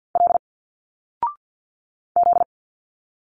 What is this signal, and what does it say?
Morse code, heard as a keyed tone, about 50 wpm, 700 Hz. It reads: L Z
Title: Single Letters 50wpm